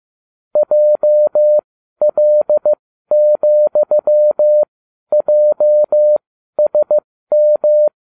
Morse code, heard as a keyed tone, about 15 wpm, spaced as for 12 wpm, 600 Hz. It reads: JL,JSM